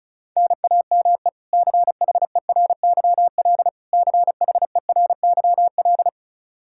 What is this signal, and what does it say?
Morse code, heard as a keyed tone, about 36 wpm, 700 Hz. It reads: NAME CHERYL CHERYL
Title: K1USN SST QSO Practice 35wpm